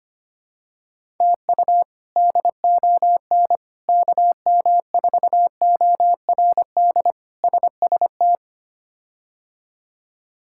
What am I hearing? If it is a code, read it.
Morse code, heard as a keyed tone, about 25 wpm, 700 Hz. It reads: TU DON KM4ORD SST
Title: K1USN SST QSO Practice - 25wpm